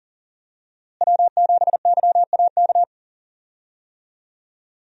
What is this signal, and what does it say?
Morse code, heard as a keyed tone, about 40 wpm, 700 Hz. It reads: W7YAK
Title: USA Call Signs - 5 Characters - Repeated 3x - 40wpm